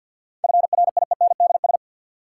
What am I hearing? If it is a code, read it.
Morse code, heard as a keyed tone, about 52 wpm, 700 Hz. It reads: FRIENDS
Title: Sentences from Top 1600 Words - New Word Drill Alt - 50wpm